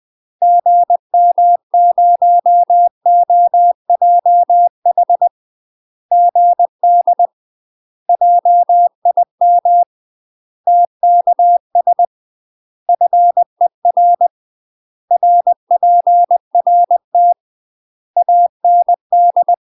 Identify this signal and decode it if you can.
Morse code, heard as a keyed tone, about 20 wpm, 700 Hz. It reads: GM0OJH GD JIM TKS FER RPRT AND